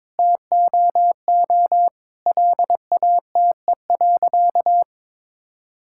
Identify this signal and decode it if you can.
Morse code, heard as a keyed tone, about 23 wpm, 700 Hz. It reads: TOO LATE.
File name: practice / 22wpm / Sentences with 3-Word Phrases - 22wpm